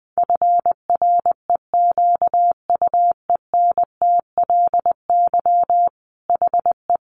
Morse code, heard as a keyed tone, about 20 wpm, 700 Hz. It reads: FREQUENTLY HE
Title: Sentences from Top 1300 Words 20wpm